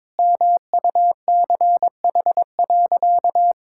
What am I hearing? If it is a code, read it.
Morse code, heard as a keyed tone, about 21 wpm, 700 Hz. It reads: MUCH.
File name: practice / 22wpm / Sentences with 2-Word Phrases - 22wpm